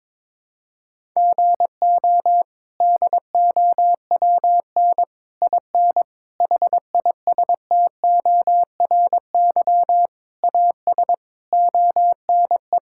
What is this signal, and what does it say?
Morse code, heard as a keyed tone, about 22 wpm, 700 Hz. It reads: GO DOWN IN HISTORY AS ONE
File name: practice / 22wpm / Sets of 8 Words - 22wpm